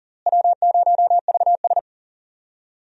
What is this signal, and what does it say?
Morse code, heard as a keyed tone, about 40 wpm, 700 Hz. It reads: W0VS
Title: USA Call Signs - 4 Characters - Repeated 2x - 40wpm